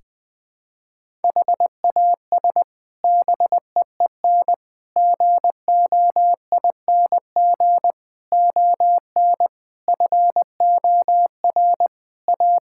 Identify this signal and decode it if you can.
Morse code, heard as a keyed tone, about 20 wpm, 700 Hz. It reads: HAS BEEN GOING ON FOR A
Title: Sets of 8 Words - 20wpm